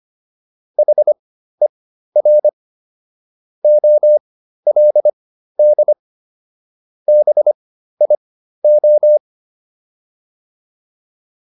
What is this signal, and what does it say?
Morse code, heard as a keyed tone, about 25 wpm, spaced as for 13 wpm, 600 Hz. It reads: HER OLD BIO
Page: intermediate-practice-files